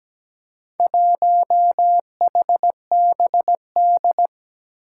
Morse code, heard as a keyed tone, about 17 wpm, 700 Hz. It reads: 1HBD
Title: Four Challenging Letter-Number Combinations 17wpm